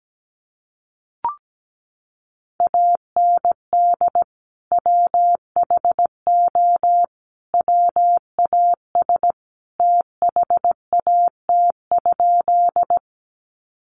Morse code, heard as a keyed tone, about 17 wpm, 700 Hz. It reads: AND WHO WAS THAT?
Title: Sentences from Top 100 Words 17wpm